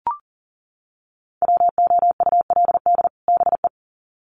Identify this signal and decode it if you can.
Morse code, heard as a keyed tone, about 40 wpm, 700 Hz. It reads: WOULD BE
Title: Sets of 2 Words 40wpm